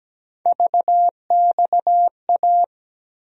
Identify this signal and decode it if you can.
Morse code, heard as a keyed tone, about 17 wpm, 700 Hz. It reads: VXA
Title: USA Call Sign Postfixes - 3 Letters 17wpm